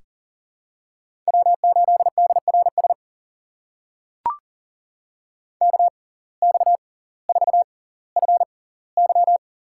Morse code, heard as a keyed tone, about 40 wpm, spaced as for 20 wpm, 700 Hz. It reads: W8DRS KX4FY
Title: USA Call Signs - 5 Characters - 6x Character Spacing - 40wpm